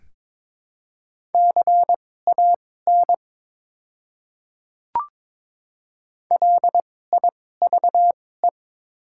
Morse code, heard as a keyed tone, about 22 wpm, spaced as for 16 wpm, 700 Hz. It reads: CAN LIVE
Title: 2x Character Spacing - Top 200 Words - 22wpm